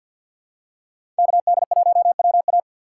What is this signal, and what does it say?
Morse code, heard as a keyed tone, about 53 wpm, 700 Hz. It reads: KD1WA
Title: USA Call Signs - 5 Characters - 6x Character Spacing - 50wpm